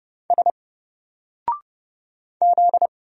Morse code, heard as a keyed tone, about 31 wpm, 700 Hz. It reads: S Z
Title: Single Letter-Number 30wpm